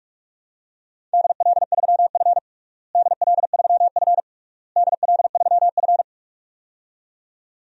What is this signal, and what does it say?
Morse code, heard as a keyed tone, about 47 wpm, 700 Hz. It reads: DL3F DL3F DL3F
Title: DXCC-100 Call Signs - 4 Characters - Repeated 3x - 45wpm